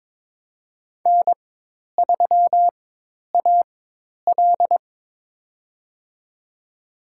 Morse code, heard as a keyed tone, about 22 wpm, spaced as for 10 wpm, 700 Hz. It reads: N3AL
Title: USA Call Signs - 4 Characters - 4x Character Spacing - 22wpm